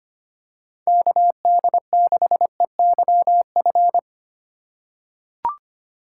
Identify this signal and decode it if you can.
Morse code, heard as a keyed tone, about 25 wpm, 700 Hz. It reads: KD6EYF
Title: USA Call Signs - 6 Characters - 6x Character Spacing - 25wpm